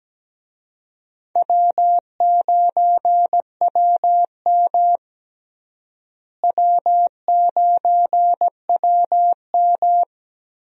Morse code, heard as a keyed tone, about 17 wpm, 700 Hz. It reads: W9WM W9WM
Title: USA Call Signs - 4 Characters - Repeated 3x - 17wpm